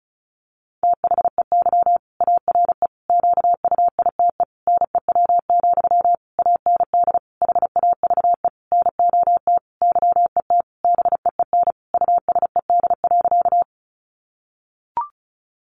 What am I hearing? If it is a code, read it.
Morse code, heard as a keyed tone, about 34 wpm, 700 Hz. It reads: THEY ARE QUITE NEW, AND HAVE NOT YET BEEN USED.
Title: Sentences from Top 700 Words 35wpm